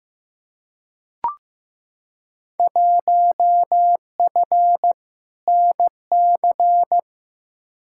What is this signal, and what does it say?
Morse code, heard as a keyed tone, about 15 wpm, 700 Hz. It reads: 1F NC
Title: Field Day Exchanges - 15wpm